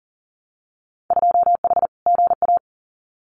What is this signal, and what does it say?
Morse code, heard as a keyed tone, about 40 wpm, 700 Hz. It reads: 2H GA
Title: Winter Field Day Exchanges Encore 40wpm